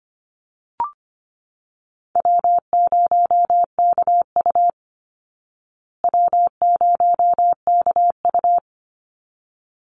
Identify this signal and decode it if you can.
Morse code, heard as a keyed tone, about 25 wpm, 700 Hz. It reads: W0KU W0KU
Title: Easy Call Signs Repeated 2x - 25wpm